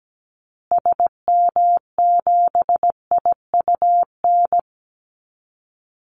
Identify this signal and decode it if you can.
Morse code, heard as a keyed tone, about 17 wpm, 700 Hz. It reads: SM7IUN
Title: Call Signs Repeated 3x - 17wpm